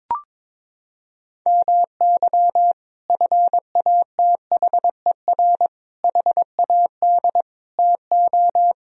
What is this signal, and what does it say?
Morse code, heard as a keyed tone, about 22 wpm, 700 Hz. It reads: MY FATHER HAD TO
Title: Sentences from Top 1000 Words - Review Part 1 - 22wpm